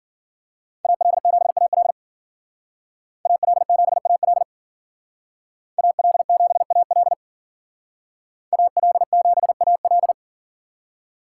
Morse code, heard as a keyed tone, about 50 wpm, 700 Hz. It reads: AL7AL AL7AL AL7AL AL7AL
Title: Speed Racing - USA 2x2 Call Signs 40wpm